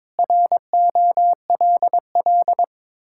Morse code, heard as a keyed tone, about 22 wpm, 700 Hz. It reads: ROLL
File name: practice / 22wpm / Sets of 7 Words - 22wpm